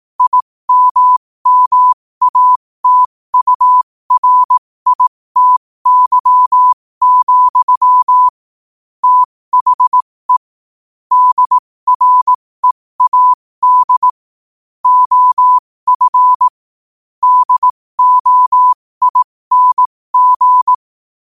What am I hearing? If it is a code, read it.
Morse code, heard as a keyed tone, about 18 wpm, spaced as for 15 wpm, 1000 Hz. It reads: IMMATURITY, THE DREAD OF DOING
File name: qotdfiles / 15wpm